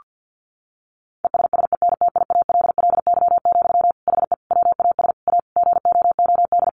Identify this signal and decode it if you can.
Morse code, heard as a keyed tone, about 48 wpm, 700 Hz. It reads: ESSENTIALLY, HE WAS A GOOD
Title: Sentences from Top 1300 Words 50wpm